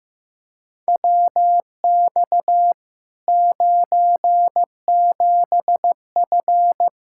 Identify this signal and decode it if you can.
Morse code, heard as a keyed tone, about 15 wpm, 700 Hz. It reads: WX 97F
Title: Practice QSO - Exchange 2 - Info - 15wpm